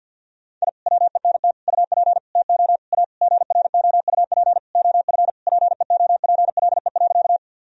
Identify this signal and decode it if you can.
Morse code, heard as a keyed tone, about 48 wpm, 700 Hz. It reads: I WENT UP TO A GROUP OF PEOPLE.
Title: Sentences from Top 1000 Words - Review Part 1 - 50wpm